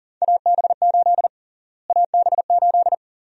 Morse code, heard as a keyed tone, about 40 wpm, 700 Hz. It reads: AB8 AB8
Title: USA Call Sign Prefixes - Repeated 2x - 40wpm